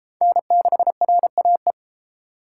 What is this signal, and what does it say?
Morse code, heard as a keyed tone, about 33 wpm, 700 Hz. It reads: N6RAE